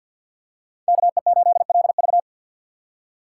Morse code, heard as a keyed tone, about 52 wpm, 700 Hz. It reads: KE8LV
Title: USA Call Signs - 5 Characters - 3x Character Spacing - 50wpm